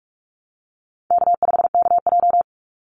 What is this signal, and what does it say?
Morse code, heard as a keyed tone, about 44 wpm, 700 Hz. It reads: K5KJ
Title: Easy Call Signs Repeated 2x - 45wpm